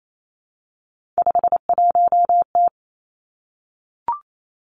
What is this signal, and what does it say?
Morse code, heard as a keyed tone, about 28 wpm, 700 Hz. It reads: 51T
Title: Three Challenging Letter-Number Combinations 28wpm